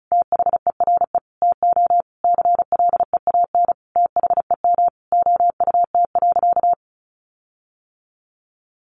Morse code, heard as a keyed tone, about 34 wpm, 700 Hz. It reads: THERE TO CLEAN THEM OUT.
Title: Sentences from Top 1500 Words 35wpm